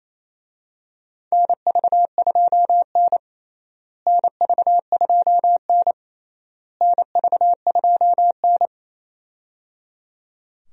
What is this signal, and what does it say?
Morse code, heard as a keyed tone, about 28 wpm, 700 Hz. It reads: NV2N NV2N NV2N
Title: USA Call Signs - 4 Characters - Repeated 3x - 28wpm